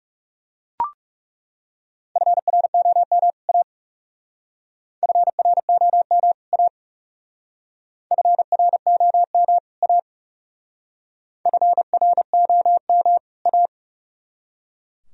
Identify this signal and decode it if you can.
Morse code, heard as a keyed tone, about 36 wpm, 700 Hz. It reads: FROM A FROM A FROM A FROM A
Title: Speed-Racing - Sets of 2 Words 30wpm